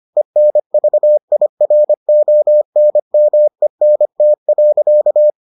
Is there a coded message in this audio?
Yes: Morse code from 0.2 to 5.3 s, heard as a keyed tone, about 25 wpm, 600 Hz. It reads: ENVIRONMENT.